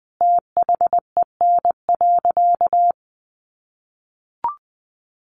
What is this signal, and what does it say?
Morse code, heard as a keyed tone, about 20 wpm, 700 Hz. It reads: THEN.
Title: Sentences from Top 200 Words 20wpm